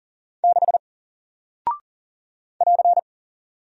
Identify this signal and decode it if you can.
Morse code, heard as a keyed tone, about 40 wpm, 700 Hz. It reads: 6 <AR>
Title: Single Character (Letter:Number:Punctuation:Prosign) Alt 40wpm